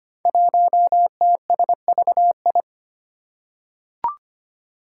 Morse code, heard as a keyed tone, about 25 wpm, 700 Hz. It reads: 1TSVI
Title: Five Challenging Letter-Number Combinations 25wpm